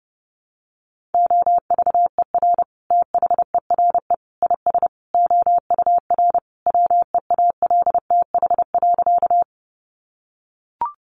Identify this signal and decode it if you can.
Morse code, heard as a keyed tone, about 30 wpm, 700 Hz. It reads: OVER THERE IS OUR WEALTH.
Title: Sentences from Top 1200 Words 30wpm